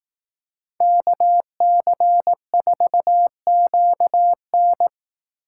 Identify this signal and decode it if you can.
Morse code, heard as a keyed tone, about 18 wpm, 700 Hz. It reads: KC4QN
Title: Speed Racing - USA Call Signs 15wpm